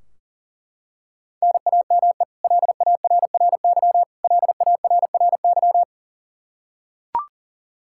Morse code, heard as a keyed tone, about 37 wpm, 700 Hz. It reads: NAME LARRY LARRY
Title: Practice QSO - Exchange 1 - Beginning - 40wpm